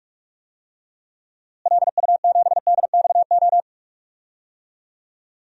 Morse code, heard as a keyed tone, about 46 wpm, 700 Hz. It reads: LU7DXO
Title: DXCC-100 Call Signs - 6 Characters - Repeated 2x - 45wpm